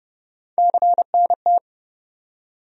Morse code, heard as a keyed tone, about 30 wpm, 700 Hz. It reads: CNT
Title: USA Call Sign Postfixes - 3 Letters 30wpm